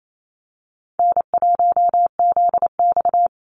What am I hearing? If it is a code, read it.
Morse code, heard as a keyed tone, about 28 wpm, 700 Hz. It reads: N1ZX
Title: Easy Call Signs Repeated 2x - 28wpm